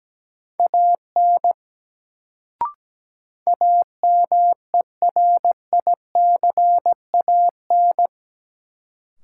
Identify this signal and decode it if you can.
Morse code, heard as a keyed tone, about 17 wpm, 700 Hz. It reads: AN AMERICAN
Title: Sets of 2 Words - New Word Drill - 17wpm